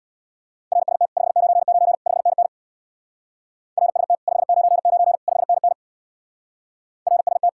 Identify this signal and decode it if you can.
Morse code, heard as a keyed tone, about 67 wpm, 700 Hz. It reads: RST 599 5NN RST 599 5NN RST